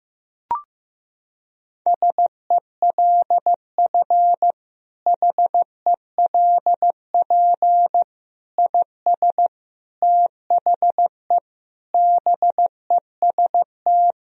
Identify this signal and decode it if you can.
Morse code, heard as a keyed tone, about 16 wpm, 700 Hz. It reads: SELF HELP IS THE BEST
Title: Sentences from Top 1000 Words - Review Part 1 - 15wpm